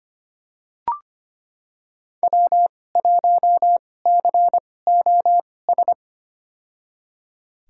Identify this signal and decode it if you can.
Morse code, heard as a keyed tone, about 25 wpm, spaced as for 18 wpm, 700 Hz. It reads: W1COS
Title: USA Call Signs - 5 Characters - 2x Character Spacing - 25wpm